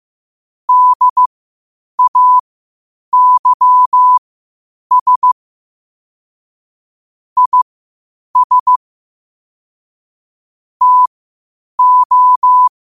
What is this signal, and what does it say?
Morse code, heard as a keyed tone, about 15 wpm, spaced as for 8 wpm, 1000 Hz. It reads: DAYS IS TO
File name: qotdfiles / 8wpm